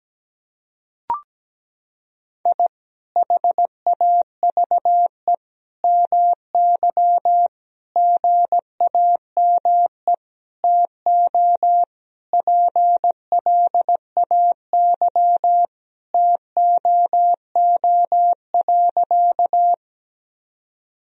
Morse code, heard as a keyed tone, about 17 wpm, 700 Hz. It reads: I HAVE MY GAME TO PLAY TOO.
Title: Sentences from Top 1000 Words - Review Part 3 - 17wpm